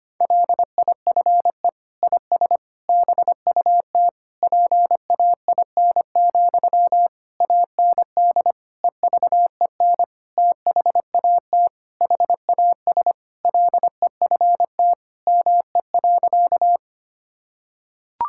Morse code, heard as a keyed tone, about 25 wpm, 700 Hz. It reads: LIFE IS BUT PAIN, AND EVEN THAT HAS LEFT ME.
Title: Sentences from Top 1600 Words 25wpm